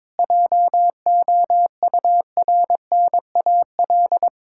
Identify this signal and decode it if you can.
Morse code, heard as a keyed tone, about 22 wpm, 700 Hz. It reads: JOURNAL
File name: Difficult Words with Hints - 22wpm